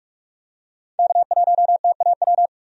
Morse code, heard as a keyed tone, about 46 wpm, 700 Hz. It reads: K1TAW
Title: USA Call Signs - 5 Characters - 45wpm